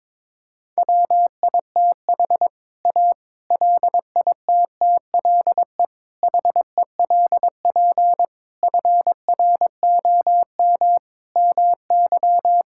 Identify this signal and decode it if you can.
Morse code, heard as a keyed tone, about 22 wpm, 700 Hz. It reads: WITH A LITTLE HELP FROM MY
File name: practice / 22wpm / Sets of 7 Words - 22wpm